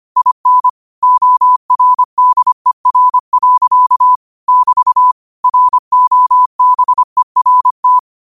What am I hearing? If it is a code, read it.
Morse code, heard as a keyed tone, about 25 wpm, 1000 Hz. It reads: IN ORDER. <BT> ROBERT